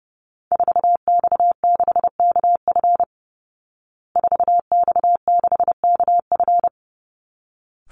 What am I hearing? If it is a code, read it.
Morse code, heard as a keyed tone, about 30 wpm, 700 Hz. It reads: 4X6KF 4X6KF
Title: Call Signs Repeated 3x - 30wpm